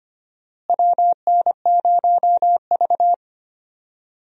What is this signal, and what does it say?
Morse code, heard as a keyed tone, about 25 wpm, 700 Hz. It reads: WN0V
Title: USA Call Signs - 4 Characters - Repeated 2x - 25wpm